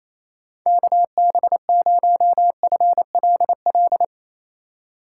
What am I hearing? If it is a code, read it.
Morse code, heard as a keyed tone, about 35 wpm, 700 Hz. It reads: KB0FLL
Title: Speed Racing - USA 2x3 Call Signs 28wpm